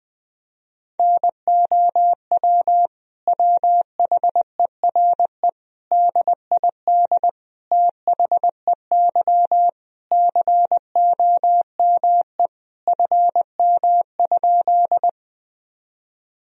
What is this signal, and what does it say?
Morse code, heard as a keyed tone, about 20 wpm, 700 Hz. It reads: NOW WHERE DID THEY COME FM?
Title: Sentences from Top 200 Words and Abbreviations 20wpm